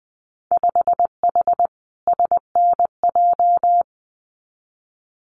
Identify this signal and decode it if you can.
Morse code, heard as a keyed tone, about 19 wpm, 700 Hz. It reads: 5H SNJ
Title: Winter Field Day Exchanges Encore 20wpm